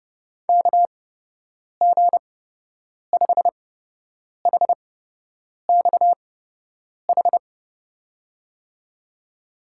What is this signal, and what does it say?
Morse code, heard as a keyed tone, about 30 wpm, spaced as for 8 wpm, 700 Hz. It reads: KG5HXH
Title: USA Call Signs - 8x Character Spacing - 30wpm